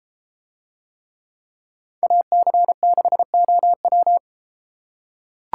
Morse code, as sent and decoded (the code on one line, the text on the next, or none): .- -.-. -.... --- .--
AC6OW